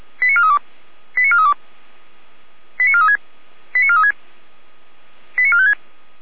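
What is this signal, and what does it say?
Five-tone selective call (ZVEI), heard as quick runs of five tones, tones of about 70 ms. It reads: 89432 89432 89436 89436 89456
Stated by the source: ZVEI five-tone selective call 89416, received by SDR, recorded by Feuerwehr Orlamünde (fire brigade).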